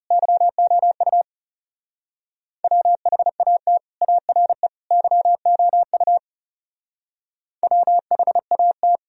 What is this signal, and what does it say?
Morse code, heard as a keyed tone, about 37 wpm, 700 Hz. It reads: YOU WHAT ARE YOU WHAT
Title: Speed-Racing - Sets of 3 Words 30wpm